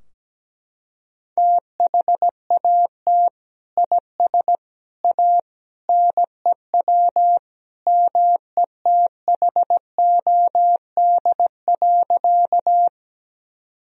Morse code, heard as a keyed tone, about 17 wpm, 700 Hz. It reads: THAT IS A NEW METHOD.
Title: Sentences from Top 1000 Words - Review Part 3 - 17wpm